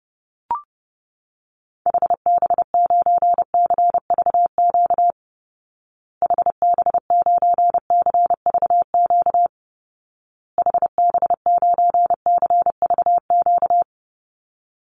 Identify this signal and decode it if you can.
Morse code, heard as a keyed tone, about 28 wpm, 700 Hz. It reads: HB9CVQ HB9CVQ HB9CVQ
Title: Call Signs Repeated 2x - 30wpm